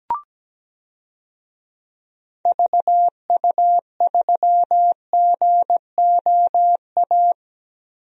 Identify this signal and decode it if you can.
Morse code, heard as a keyed tone, about 17 wpm, 700 Hz. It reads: VU3GOA
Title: DXCC-100 Call Signs - 6 Characters - Repeated 2x - 17wpm